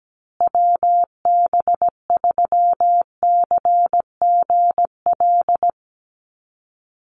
Morse code, heard as a keyed tone, about 17 wpm, 700 Hz. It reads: WB3CGL
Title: USA Call Signs Repeated 2x 17wpm